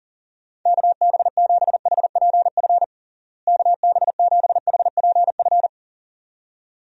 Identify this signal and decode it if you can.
Morse code, heard as a keyed tone, about 40 wpm, 700 Hz. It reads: KB7HPF KB7HPF
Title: USA Call Signs - 6 Characters - Repeated 2x - 40wpm